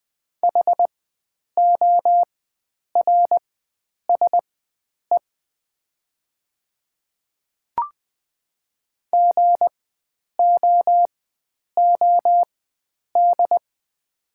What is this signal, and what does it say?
Morse code, heard as a keyed tone, about 20 wpm, spaced as for 9 wpm, 700 Hz. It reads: HORSE GOOD
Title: 4x Character Spacing - Top 300 Words - 20wpm